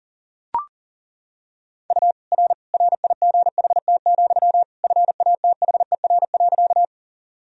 Morse code, heard as a keyed tone, about 40 wpm, 700 Hz. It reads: U R RIGHT, FATHER.
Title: Sentences from Top 300 Words and Abbreviations 40wpm